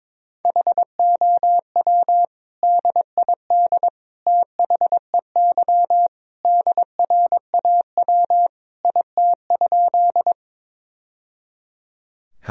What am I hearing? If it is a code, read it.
Morse code, heard as a keyed tone, about 22 wpm, 700 Hz. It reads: HOW DID THEY DRAW IT?
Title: Sentences from Top 1000 Words - Review Part 1 - 22wpm